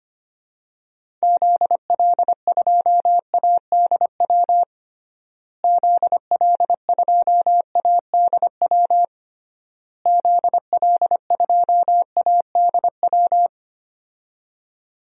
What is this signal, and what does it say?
Morse code, heard as a keyed tone, about 25 wpm, 700 Hz. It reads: ZL2ADW ZL2ADW ZL2ADW
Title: DXCC-100 Call Signs - 6 Characters - Repeated 3x - 25wpm